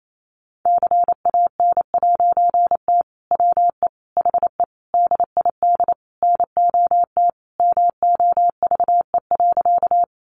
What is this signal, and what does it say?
Morse code, heard as a keyed tone, about 28 wpm, 700 Hz. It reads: CAN'T WE HE DID NOT MOVE.
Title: A Sound Of Thunder 28wpm